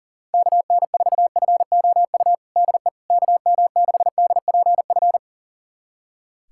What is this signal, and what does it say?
Morse code, heard as a keyed tone, about 40 wpm, 700 Hz. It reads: KN4FOU DE KM6DPF
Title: USA Callsign DE Callsign - 6 Characters - 40wpm